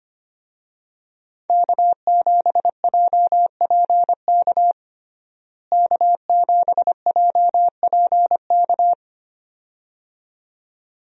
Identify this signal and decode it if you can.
Morse code, heard as a keyed tone, about 25 wpm, 700 Hz. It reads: K7JPK K7JPK
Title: USA Call Signs - 5 Characters - Repeated 2x - 25wpm